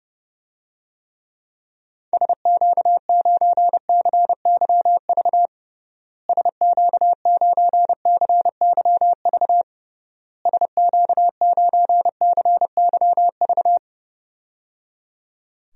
Morse code, heard as a keyed tone, about 29 wpm, 700 Hz. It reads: SQ9CYV SQ9CYV SQ9CYV
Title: DXCC-100 Call Signs - 6 Characters - Repeated 3x - 30wpm